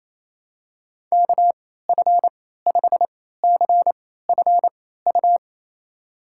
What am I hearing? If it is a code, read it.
Morse code, heard as a keyed tone, about 28 wpm, spaced as for 16 wpm, 700 Hz. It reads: KF5CFU
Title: USA Call Signs - 6 Characters - 3x Character Spacing - 28wpm